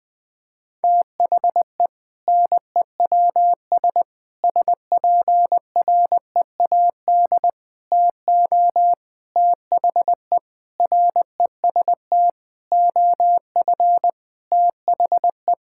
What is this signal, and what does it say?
Morse code, heard as a keyed tone, about 20 wpm, 700 Hz. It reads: THE NEWS SPREAD TO THE REST OF THE
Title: Sentences with 5-Word Phrases - 20wpm